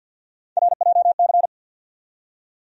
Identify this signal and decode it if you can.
Morse code, heard as a keyed tone, about 50 wpm, 700 Hz. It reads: RJC
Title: USA Call Sign Postfixes - 3 Letters 50wpm